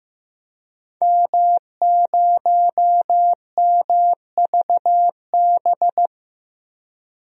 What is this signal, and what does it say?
Morse code, heard as a keyed tone, about 15 wpm, 700 Hz. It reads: M0MVB
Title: DXCC-100 Call Signs - 5 Characters - 15wpm